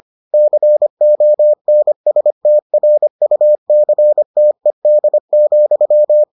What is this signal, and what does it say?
Morse code, heard as a keyed tone, about 25 wpm, 600 Hz. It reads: CONSTRUCTED,